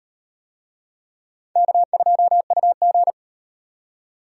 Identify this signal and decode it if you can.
Morse code, heard as a keyed tone, about 38 wpm, 700 Hz. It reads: K2UG